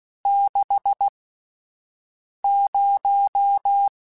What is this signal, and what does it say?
Morse code, heard as a keyed tone, about 16 wpm, spaced as for 5 wpm, 800 Hz. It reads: 60
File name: Spaced Characters / Spaced Characters-mp3s / Weather 5 22 sp @10